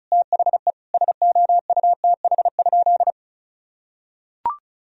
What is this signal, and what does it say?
Morse code, heard as a keyed tone, about 34 wpm, 700 Hz. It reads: THE SOUTH?
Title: Sentences from Top 1000 Words - Review Part 2 - 35wpm